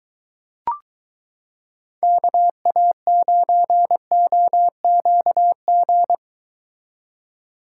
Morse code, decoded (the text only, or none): KA9OQG